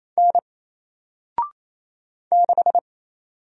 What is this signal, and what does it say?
Morse code, heard as a keyed tone, about 27 wpm, 700 Hz. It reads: N 6
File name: Single-Letter-Number-28wpm